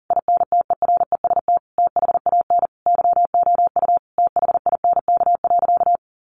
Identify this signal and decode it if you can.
Morse code, heard as a keyed tone, about 40 wpm, 700 Hz. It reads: INTEREST THAN YOU THINK.
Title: Sentences from Top 400 Words 40wpm